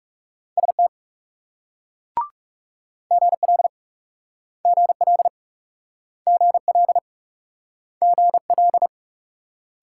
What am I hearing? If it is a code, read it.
Morse code, heard as a keyed tone, about 36 wpm, 700 Hz. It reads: ST GL GL GL GL
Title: Speed-Racing Multi-letter Phonemes - 30wpm